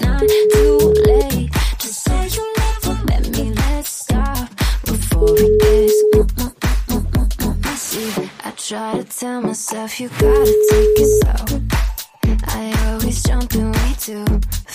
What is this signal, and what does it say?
Ringback tone (Europe, 425 Hz), with noise.